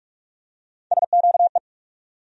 Morse code, heard as a keyed tone, about 46 wpm, 700 Hz. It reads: SQE
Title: USA Call Sign Postfixes - 3 Letters 45wpm